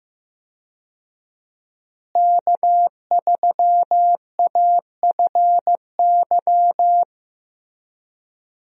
Morse code, heard as a keyed tone, about 15 wpm, 700 Hz. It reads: K3AFY
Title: USA Call Signs - 5 Characters - Repeated 2x - 15wpm